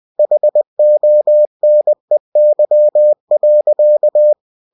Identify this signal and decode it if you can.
Morse code, heard as a keyed tone, about 20 wpm, 600 Hz. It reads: HONEY.